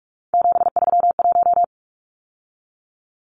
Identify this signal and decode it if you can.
Morse code, heard as a keyed tone, about 44 wpm, 700 Hz. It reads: 731
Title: Numbers - 3-Digit 45wpm